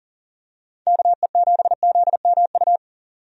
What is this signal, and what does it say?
Morse code, heard as a keyed tone, about 40 wpm, 700 Hz. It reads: KE7ZMU
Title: USA Call Signs - 6 Characters - 40wpm